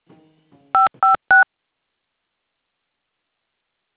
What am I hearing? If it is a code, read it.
DTMF, heard as telephone touch tones, tones of about 140 ms. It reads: 556
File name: things / adit fxo tellme